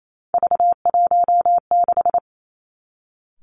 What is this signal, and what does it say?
Morse code, heard as a keyed tone, about 28 wpm, 700 Hz. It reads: V16
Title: Three Challenging Letter-Number Combinations 28wpm